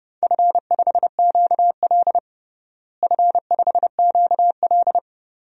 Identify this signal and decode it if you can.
Morse code, heard as a keyed tone, about 31 wpm, 700 Hz. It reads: F5QL F5QL
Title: DXCC-100 Call Signs - 4 Characters - Repeated 3x - 30wpm